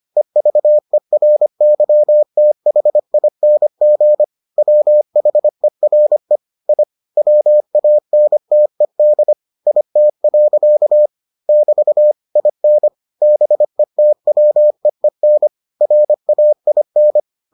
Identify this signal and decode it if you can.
Morse code, heard as a keyed tone, about 25 wpm, 600 Hz. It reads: EVERYTHING WHERE I WANTED IT. <BT> IN BETWEEN RAIN